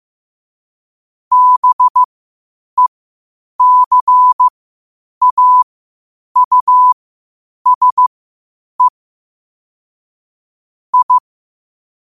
Morse code, heard as a keyed tone, about 15 wpm, spaced as for 8 wpm, 1000 Hz. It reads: BECAUSE I
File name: qotdfiles / 8wpm